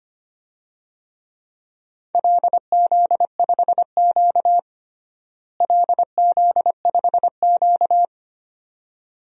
Morse code, heard as a keyed tone, about 25 wpm, 700 Hz. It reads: LZ5Q LZ5Q
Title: DXCC-100 Call Signs - 4 Characters - Repeated 2x - 25wpm